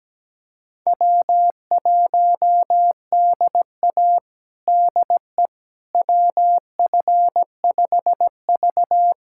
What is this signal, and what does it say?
Morse code, heard as a keyed tone, about 17 wpm, 700 Hz. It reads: W1DA DE WF5V
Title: USA Callsign DE Callsign - 4 Characters - 17wpm